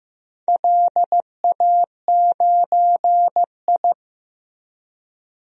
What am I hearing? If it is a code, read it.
Morse code, heard as a keyed tone, about 16 wpm, 700 Hz. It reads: LA9I
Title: DXCC-100 Call Signs - 4 Characters - Repeated 3x - 15wpm